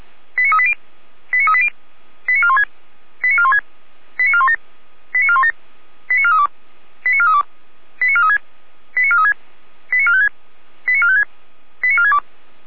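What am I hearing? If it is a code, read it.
Five-tone selective call (ZVEI), heard as quick runs of five tones, tones of about 70 ms. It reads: 89380 89380 89416 89416 89417 89417 89432 89432 89436 89436 89456 89456 89462
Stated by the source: ZVEI five-tone selective call received on SDR.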